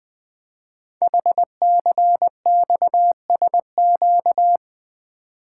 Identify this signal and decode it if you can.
Morse code, heard as a keyed tone, about 20 wpm, 700 Hz. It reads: HCXSQ
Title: Code Groups - 5 Letters - 20wpm